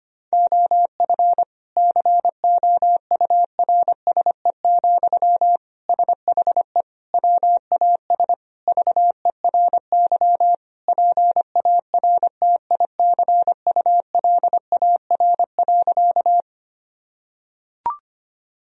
Morse code, heard as a keyed tone, about 25 wpm, 700 Hz. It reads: OF COURSE, SHE WAS VERY PARTICULAR.
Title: Sentences from Top 1000 Words - Review Part 4 - 25wpm